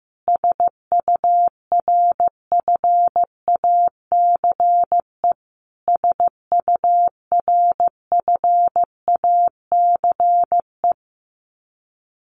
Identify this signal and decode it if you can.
Morse code, heard as a keyed tone, about 15 wpm, 700 Hz. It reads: SURFACE SURFACE
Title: Top 500 Words Repeated 15wpm